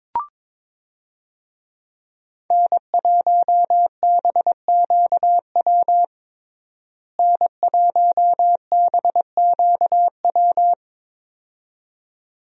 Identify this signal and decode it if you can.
Morse code, heard as a keyed tone, about 22 wpm, 700 Hz. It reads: N1BQW N1BQW
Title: USA Call Signs - 5 Characters - Repeated 2x - 22wpm